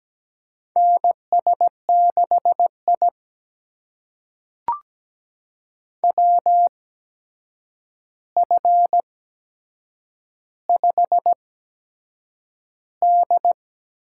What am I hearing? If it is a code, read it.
Morse code, heard as a keyed tone, about 17 wpm, spaced as for 7 wpm, 700 Hz. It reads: NS6I WF5D
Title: USA Call Signs - 4 Characters - 8x Character Spacing - 17wpm